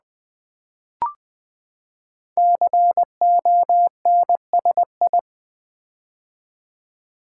Morse code, heard as a keyed tone, about 20 wpm, 700 Hz. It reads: CONSI
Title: Build A Word - 5 Letters At A Time - 12-Letter Words - 20wpm